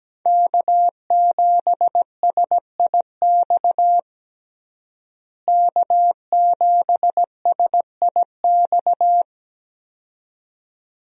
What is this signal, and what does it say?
Morse code, heard as a keyed tone, about 17 wpm, 700 Hz. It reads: K7SIX K7SIX
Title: USA Call Signs - 5 Characters - Repeated 2x - 17wpm